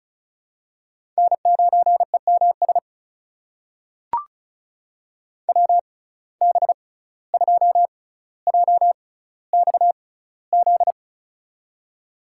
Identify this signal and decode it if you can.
Morse code, heard as a keyed tone, about 36 wpm, spaced as for 17 wpm, 700 Hz. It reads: N9EMS WB2JXZ
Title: USA Call Signs - 6x Character Spacing - 35wpm